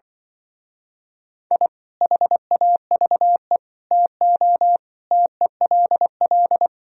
Morse code, heard as a keyed tone, about 24 wpm, 700 Hz. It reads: I HAVE TO TELL
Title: Speed-Racing - Sets of 5 Words 20wpm